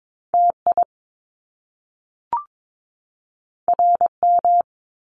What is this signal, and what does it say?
Morse code, heard as a keyed tone, about 22 wpm, 700 Hz. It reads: TI RM
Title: Top Two Letter Combinations in Words 22wpm